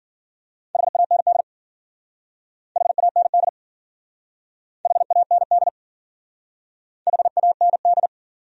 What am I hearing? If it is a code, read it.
Morse code, heard as a keyed tone, about 49 wpm, 700 Hz. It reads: HAND HAND HAND HAND
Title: Speed-Racing Top 500 Words 40wpm